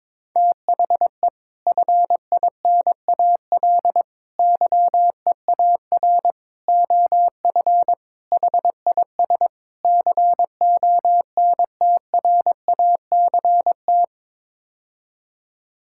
Morse code, heard as a keyed tone, about 22 wpm, 700 Hz. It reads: THE FINAL YEAR OF HIS CONTRACT
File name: practice / 22wpm / Sets of 6 Words - 22wpm